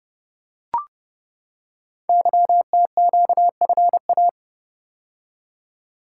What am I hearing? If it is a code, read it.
Morse code, heard as a keyed tone, about 30 wpm, 690 Hz. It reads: YTQFA
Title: Code Groups - 5 Letters - 30wpm